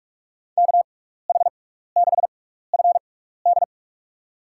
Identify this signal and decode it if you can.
Morse code, heard as a keyed tone, about 43 wpm, spaced as for 15 wpm, 690 Hz. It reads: KH6FD
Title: USA Call Signs - 6x Character Spacing - 45wpm